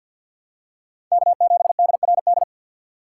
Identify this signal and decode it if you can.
Morse code, heard as a keyed tone, about 46 wpm, 700 Hz. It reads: K7DRD
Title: USA Call Signs - 3x Character Spacing - 50wpm